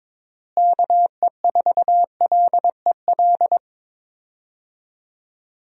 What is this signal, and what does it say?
Morse code, heard as a keyed tone, about 22 wpm, 700 Hz. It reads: KE4LEL
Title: USA Call Signs - 6 Characters - Repeated 3x - 22wpm